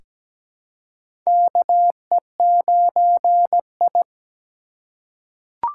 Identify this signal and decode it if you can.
Morse code, heard as a keyed tone, about 17 wpm, 700 Hz. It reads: KE9I
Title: K1USN SST QSO Practice - 17wpm